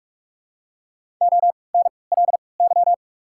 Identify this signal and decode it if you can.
Morse code, heard as a keyed tone, about 48 wpm, 700 Hz. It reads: O N L Y
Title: 3x Character Spacing - Top 500 Words - 45wpm